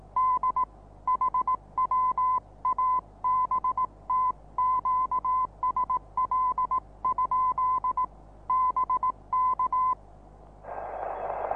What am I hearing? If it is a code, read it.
Morse code, heard as a keyed tone, about 18 wpm, 1020 Hz. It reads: D HW ABT QSL? BK
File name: repos / Morsekurs / DJ4UF Morsen26